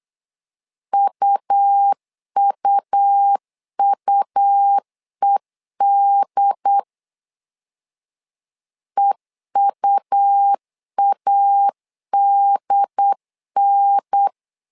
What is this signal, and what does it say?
Morse code, heard as a keyed tone, about 8 wpm, 800 Hz. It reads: UUUED EUADN
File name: LEZIONI / 3 MWRO / 3 M 8wpm 2300